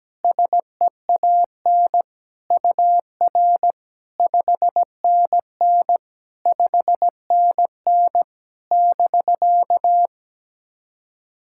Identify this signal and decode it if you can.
Morse code, heard as a keyed tone, about 17 wpm, 700 Hz. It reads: SEAN UR 5NN 5NN <BK>
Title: SOTA QSO Practice 17wpm